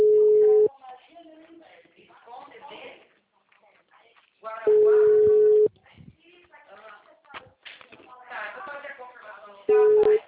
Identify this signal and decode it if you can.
Ringback tone (Europe, 425 Hz), with speech.